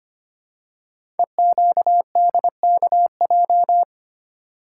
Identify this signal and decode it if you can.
Morse code, heard as a keyed tone, about 25 wpm, 700 Hz. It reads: EQDKJ
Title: Code Groups - 5 Letters - 25wpm